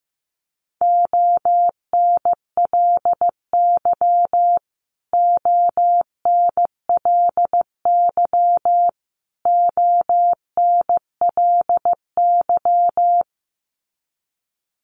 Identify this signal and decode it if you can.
Morse code, heard as a keyed tone, about 15 wpm, 700 Hz. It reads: ONLY ONLY ONLY
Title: Top 200 Words Repeated 15wpm